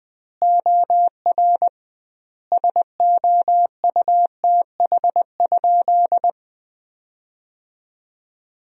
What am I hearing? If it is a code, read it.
Morse code, heard as a keyed tone, about 20 wpm, 700 Hz. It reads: OR SOUTH?
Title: 2x Word-Spacing - Sentences from Top 400 Words - 20wpm